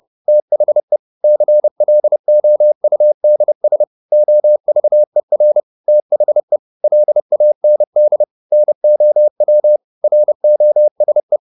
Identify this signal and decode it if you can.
Morse code, heard as a keyed tone, about 30 wpm, 600 Hz. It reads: THE CLOUDS OVER THE LAND NOW ROSE
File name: the old man and the sea / the old man and the sea 1 30wpm